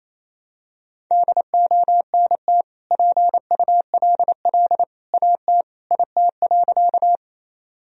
Morse code, heard as a keyed tone, about 28 wpm, 700 Hz. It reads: DONT PULL AT IT.
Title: Sentences from Top 1000 Words - Review Part 2 - 28wpm